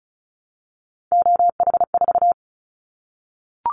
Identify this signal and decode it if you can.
Morse code, heard as a keyed tone, about 34 wpm, 700 Hz. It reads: OH4
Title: Callsign Prefixes for Top 25 DXCC - Repeated 3x - 35wpm